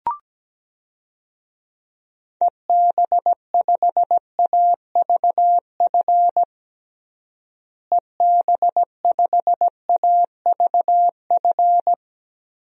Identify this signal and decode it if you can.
Morse code, heard as a keyed tone, about 17 wpm, 700 Hz. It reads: EB5AVF EB5AVF
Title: DXCC-100 Call Signs - 6 Characters - Repeated 2x - 17wpm